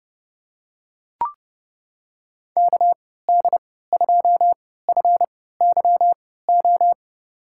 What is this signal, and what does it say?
Morse code, heard as a keyed tone, about 31 wpm, spaced as for 17 wpm, 700 Hz. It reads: KD2FYO
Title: USA Call Signs - 6 Characters - 3x Character Spacing - 30wpm